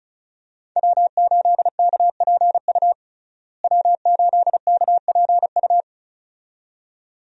Morse code, heard as a keyed tone, about 34 wpm, 700 Hz. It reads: W8KPU W8KPU
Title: USA Call Signs - 5 Characters - Repeated 2x - 35wpm